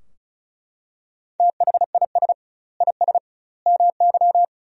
Morse code, heard as a keyed tone, about 36 wpm, 700 Hz. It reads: THIS IS MY
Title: 2x Word-Spacing - Sentences from Top 200 Words - 35wpm